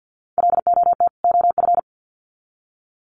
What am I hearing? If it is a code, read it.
Morse code, heard as a keyed tone, about 50 wpm, 700 Hz. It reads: LOT OF
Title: Sets of 2 Words 50wpm